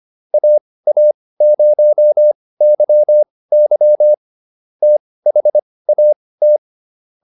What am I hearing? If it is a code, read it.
Morse code, heard as a keyed tone, about 25 wpm, spaced as for 18 wpm, 600 Hz. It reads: AA0YY THAT